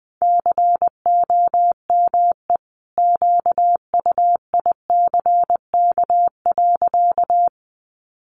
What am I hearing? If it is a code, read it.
Morse code, heard as a keyed tone, about 20 wpm, 700 Hz. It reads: COME QUICK.
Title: Sentences from Top 500 Words 20wpm